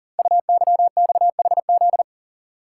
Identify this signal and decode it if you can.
Morse code, heard as a keyed tone, about 38 wpm, 690 Hz. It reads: UYXHZ
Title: Code Groups - 5 Letters - 40wpm